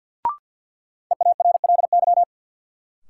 Morse code, heard as a keyed tone, about 47 wpm, 700 Hz. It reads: EARLY
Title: Sets of 3 Words - New Word Drill - 50wpm